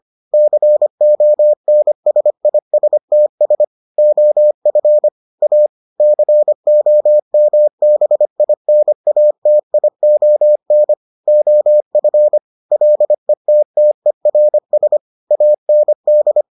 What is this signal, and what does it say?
Morse code, heard as a keyed tone, about 25 wpm, 600 Hz. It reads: CONSISTS OF A COMBINATION OF LETTERS AND